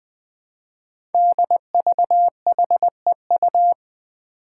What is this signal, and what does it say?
Morse code, heard as a keyed tone, about 21 wpm, 700 Hz. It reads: DVHEU
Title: Five Challenging Letter-Number Combinations 20wpm